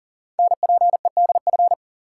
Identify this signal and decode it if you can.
Morse code, heard as a keyed tone, about 41 wpm, 700 Hz. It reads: NPEDF
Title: Code Groups - 5 Letter:Numbers - 40wpm